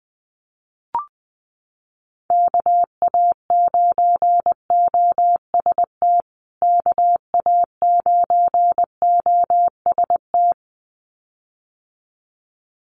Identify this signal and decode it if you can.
Morse code, heard as a keyed tone, about 19 wpm, 700 Hz. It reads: KA9OST KA9OST
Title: USA Call Signs Repeated 2x 20wpm